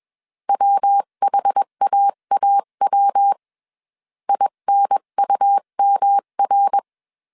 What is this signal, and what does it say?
Morse code, heard as a keyed tone, about 21 wpm, 800 Hz. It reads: WHAAW INUMR
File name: 20wpm